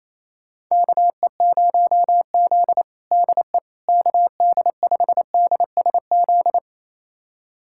Morse code, heard as a keyed tone, about 28 wpm, 700 Hz. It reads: KE0Z DE KD5DSZ
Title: USA Callsign DE Callsign 28wpm